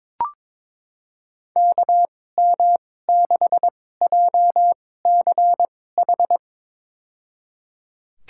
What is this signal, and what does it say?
Morse code, heard as a keyed tone, about 22 wpm, spaced as for 16 wpm, 700 Hz. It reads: KM6JCH
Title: USA Call Signs - 6 Characters - 2x Character Spacing - 22wpm